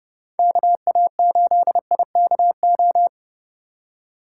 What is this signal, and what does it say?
Morse code, heard as a keyed tone, about 30 wpm, 700 Hz. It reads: KA8IKO